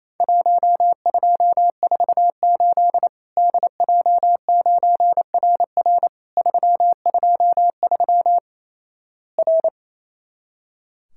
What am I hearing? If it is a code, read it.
Morse code, heard as a keyed tone, about 28 wpm, 700 Hz. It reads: 1248 DJ9RR 323 R
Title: Worked All Europe DX Contest - QTC Exchange - 28wpm